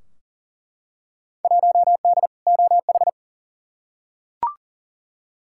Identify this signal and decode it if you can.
Morse code, heard as a keyed tone, about 40 wpm, 700 Hz. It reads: 1D OH
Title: Field Day Exchanges - 40wpm